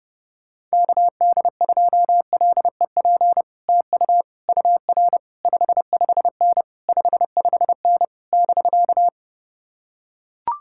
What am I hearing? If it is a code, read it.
Morse code, heard as a keyed tone, about 30 wpm, 700 Hz. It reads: KD2LEP TU UR 55N 55N <BK>